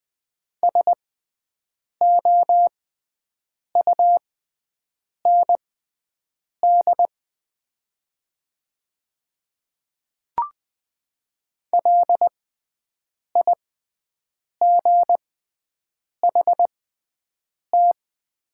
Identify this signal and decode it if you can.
Morse code, heard as a keyed tone, about 20 wpm, spaced as for 7 wpm, 700 Hz. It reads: SOUND LIGHT
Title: 6x Character Spacing - Top 300 Words - 20wpm